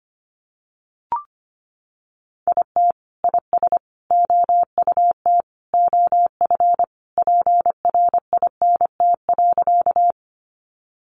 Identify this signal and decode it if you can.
Morse code, heard as a keyed tone, about 25 wpm, 700 Hz. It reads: IT IS OUT OF PRINT.
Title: Sentences from Top 1400 Words 25wpm